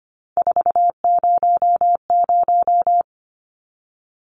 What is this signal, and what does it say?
Morse code, heard as a keyed tone, about 25 wpm, 700 Hz. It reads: 400
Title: Numbers - 3-Digit 25wpm